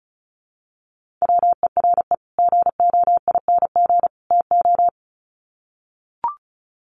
Morse code, heard as a keyed tone, about 34 wpm, 700 Hz. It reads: WERE GOING TO
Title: Sets of 3 Words 35wpm